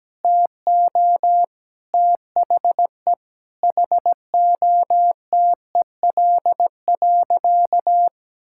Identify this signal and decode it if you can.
Morse code, heard as a keyed tone, about 17 wpm, 700 Hz. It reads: TO THE HOTEL.
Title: Sentences from Top 1600 Words 17wpm